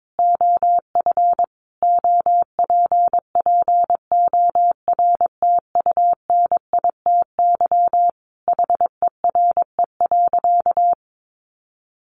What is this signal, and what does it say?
Morse code, heard as a keyed tone, about 22 wpm, 700 Hz. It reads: OF OPPORTUNITY HERE.
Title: Sentences from Top 1300 Words 22wpm